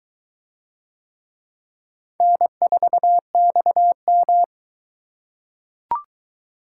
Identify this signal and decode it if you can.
Morse code, heard as a keyed tone, about 22 wpm, 700 Hz. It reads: N4XM